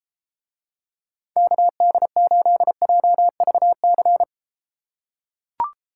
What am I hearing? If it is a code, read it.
Morse code, heard as a keyed tone, about 34 wpm, 700 Hz. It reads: KD8JVC